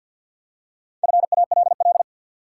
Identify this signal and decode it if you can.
Morse code, heard as a keyed tone, about 51 wpm, 700 Hz. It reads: FALL
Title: Four Letter Words - 50wpm